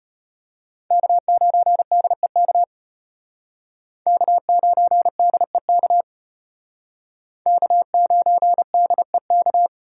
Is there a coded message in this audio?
Yes: Morse code from 0.9 to 9.7 s, heard as a keyed tone, about 32 wpm, 700 Hz. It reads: K9DEK K9DEK K9DEK